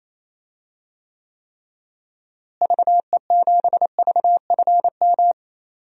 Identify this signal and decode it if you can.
Morse code, heard as a keyed tone, about 28 wpm, 700 Hz. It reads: VE7VFM
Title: DXCC-100 Call Signs - 6 Characters - Repeated 3x - 28wpm